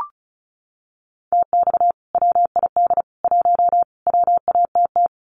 Morse code, heard as a keyed tone, about 34 wpm, 700 Hz. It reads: TX WID 1 WATT
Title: Common QSO Phrases 35wpm